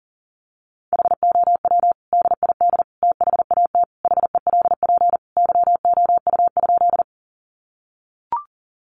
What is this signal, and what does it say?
Morse code, heard as a keyed tone, about 40 wpm, 700 Hz. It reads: HOW DID THAT HELP YOU?
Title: Sentences from Top 200 Words 40wpm